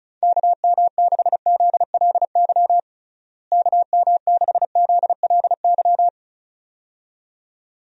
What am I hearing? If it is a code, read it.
Morse code, heard as a keyed tone, about 36 wpm, 700 Hz. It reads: KM6ZLY KM6ZLY
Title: USA Call Signs - 6 Characters - Repeated 2x - 35wpm